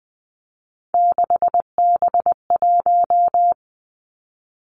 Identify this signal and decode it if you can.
Morse code, heard as a keyed tone, about 20 wpm, 700 Hz. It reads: 6B1
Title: Three Challenging Letter-Number Combinations 20wpm